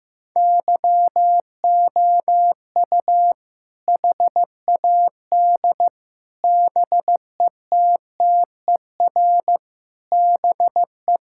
Morse code, heard as a keyed tone, about 14 wpm, 700 Hz. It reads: YOU HAD BETTER BE
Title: Sentences from Top 1600 Words 15wpm